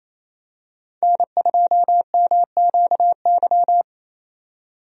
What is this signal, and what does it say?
Morse code, heard as a keyed tone, about 27 wpm, 700 Hz. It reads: N2MQY
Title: USA Call Signs - 5 Characters - 28wpm